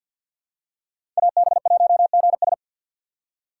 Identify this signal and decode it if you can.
Morse code, heard as a keyed tone, about 52 wpm, 700 Hz. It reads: AB1GS
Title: USA Call Signs - 5 Characters - 6x Character Spacing - 50wpm